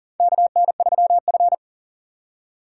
Morse code, heard as a keyed tone, about 39 wpm, 690 Hz. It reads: KN3F
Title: USA Call Signs - 2x Character Spacing - 40wpm